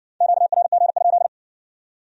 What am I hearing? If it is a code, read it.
Morse code, heard as a keyed tone, about 55 wpm, 700 Hz. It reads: QRG?